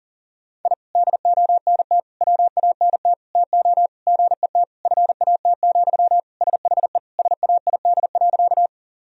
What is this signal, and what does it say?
Morse code, heard as a keyed tone, about 40 wpm, 700 Hz. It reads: I DONT WANT TO GET FAT, SHE SAID.
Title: Sentences from Top 1000 Words - Review Part 4 - 40wpm